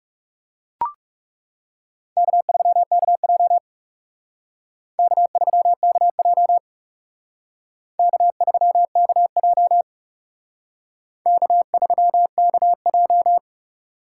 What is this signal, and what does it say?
Morse code, heard as a keyed tone, about 37 wpm, 700 Hz. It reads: K3KJ K3KJ K3KJ K3KJ
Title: Speed Racing - USA 1x2 Call Signs 30wpm